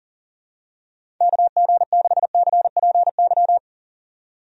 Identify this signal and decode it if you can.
Morse code, heard as a keyed tone, about 37 wpm, 700 Hz. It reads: KG6CPY
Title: USA Call Signs - 6 Characters - 40wpm